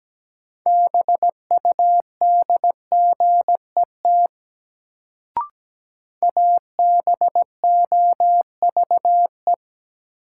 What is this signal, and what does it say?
Morse code, heard as a keyed tone, about 17 wpm, 700 Hz. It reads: BUDGET ABOVE
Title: Sentences from Top 1200 Words - New Word Drill Alt - 17wpm